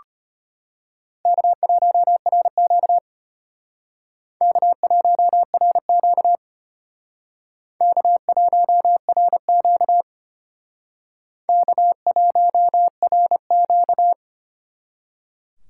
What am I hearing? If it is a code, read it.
Morse code, heard as a keyed tone, about 32 wpm, 700 Hz. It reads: K1RQ K1RQ K1RQ K1RQ